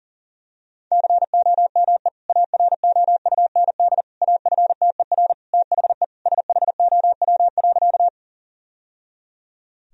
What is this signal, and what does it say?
Morse code, heard as a keyed tone, about 40 wpm, 700 Hz. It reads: COME AROUND AFTER THE SHOW.
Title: Sentences from Top 1000 Words - Review Part 3 - 40wpm